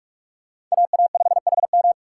Morse code, heard as a keyed tone, about 38 wpm, 700 Hz. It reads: AA5HM